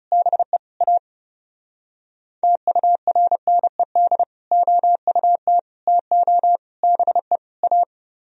Morse code, heard as a keyed tone, about 38 wpm, 700 Hz. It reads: BE A TURNED OUT TO BE A